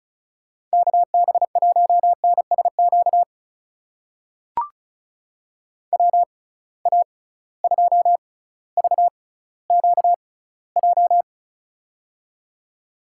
Morse code, heard as a keyed tone, about 36 wpm, spaced as for 18 wpm, 700 Hz. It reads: KB1NSQ WA2VQJ
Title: USA Call Signs - 6 Characters - 6x Character Spacing - 35wpm